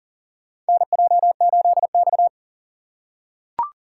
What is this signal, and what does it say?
Morse code, heard as a keyed tone, about 40 wpm, 700 Hz. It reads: NJ8X
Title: USA Call Signs - 4 Characters - 3x Character Spacing - 40wpm